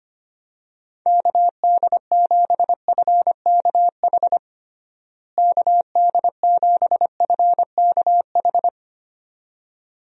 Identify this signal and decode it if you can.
Morse code, heard as a keyed tone, about 25 wpm, 700 Hz. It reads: KD7FKH KD7FKH
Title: USA Call Signs - 6 Characters - Repeated 2x - 25wpm